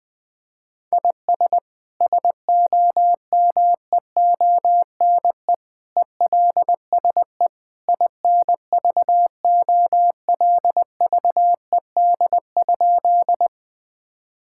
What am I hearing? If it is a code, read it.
Morse code, heard as a keyed tone, about 21 wpm, 700 Hz. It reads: IS SOMEONE ELSE INVOLVED?
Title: Sentences from Top 1000 Words - Review Part 5 - 20wpm